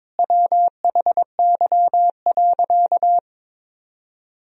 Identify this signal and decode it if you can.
Morse code, heard as a keyed tone, about 22 wpm, 700 Hz. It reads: WHY.
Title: Sentences from Top 1000 Words - Review Part 3 - 22wpm